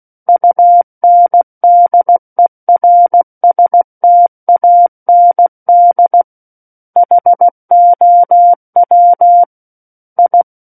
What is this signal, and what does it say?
Morse code, heard as a keyed tone, about 16 wpm, 700 Hz. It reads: UNDERSTAND HOW I
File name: Book One / 07-Chapter 7 - 16 WPM